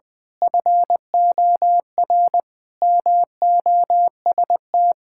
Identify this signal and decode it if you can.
Morse code, heard as a keyed tone, about 21 wpm, 700 Hz. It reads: FOR MOST
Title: Pangram Challenge v2 - 20wpm